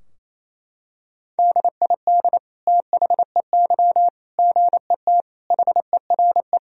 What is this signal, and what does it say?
Morse code, heard as a keyed tone, about 27 wpm, 700 Hz. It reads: DID THEY GET HERE
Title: Sentences from Top 1000 Words - Review Part 3 - 28wpm